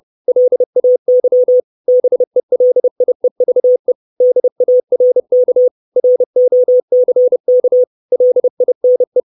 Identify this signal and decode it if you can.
Morse code, heard as a keyed tone, about 30 wpm, 500 Hz. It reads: LAY BELIEVE DARK ROCK LINE
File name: 632 Words 30WPM